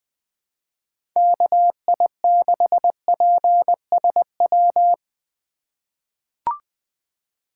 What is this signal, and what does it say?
Morse code, heard as a keyed tone, about 20 wpm, 700 Hz. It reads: KI6PSW
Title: USA Call Signs - 4x Character Spacing - 20wpm